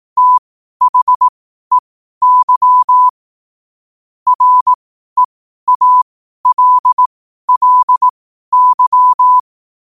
Morse code, heard as a keyed tone, about 18 wpm, spaced as for 12 wpm, 1000 Hz. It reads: THEY REALLY